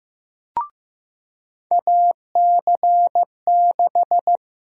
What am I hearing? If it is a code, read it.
Morse code, heard as a keyed tone, about 15 wpm, 700 Hz. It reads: AC6
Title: USA Call Sign Prefixes - Repeated 2x - 15wpm